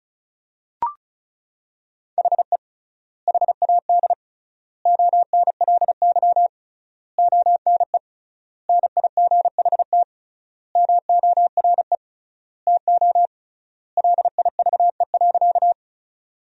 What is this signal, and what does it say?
Morse code, heard as a keyed tone, about 36 wpm, 700 Hz. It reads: HE HAD ONLY ONE NIGHT MORE TO LIVE.
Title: 3x Word-Spacing - Sentences from Top 300 Words - 35wpm